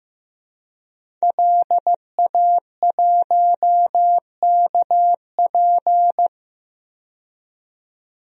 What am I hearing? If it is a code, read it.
Morse code, heard as a keyed tone, about 15 wpm, 700 Hz. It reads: LA1KP
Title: DXCC-100 Call Signs - 5 Characters - 15wpm